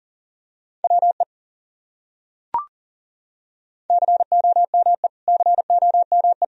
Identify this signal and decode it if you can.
Morse code, heard as a keyed tone, about 40 wpm, 700 Hz. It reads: WE COME COME
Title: Top 100 Words Repeated 40wpm